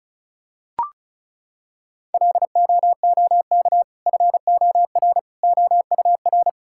Morse code, heard as a keyed tone, about 36 wpm, 700 Hz. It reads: LOOK FOR OUR
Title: Sentences from Top 1000 Words - Review Part 6 - 35wpm